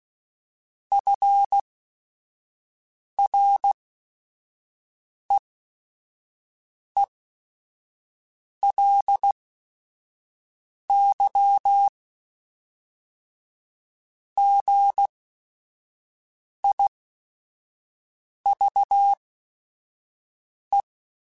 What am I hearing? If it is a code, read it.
Morse code, heard as a keyed tone, about 16 wpm, spaced as for 5 wpm, 800 Hz. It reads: FREELY GIVE
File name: Quotes / Quotes-mp3s / Advice @5